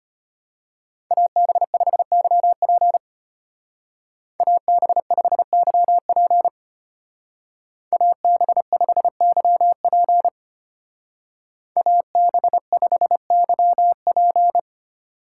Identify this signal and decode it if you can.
Morse code, heard as a keyed tone, about 32 wpm, 700 Hz. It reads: AB5YP AB5YP AB5YP AB5YP